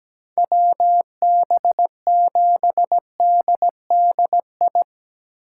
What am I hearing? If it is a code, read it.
Morse code, heard as a keyed tone, about 17 wpm, 700 Hz. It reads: WB7DDI
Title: USA Call Signs - 3x Character Spacing - 17wpm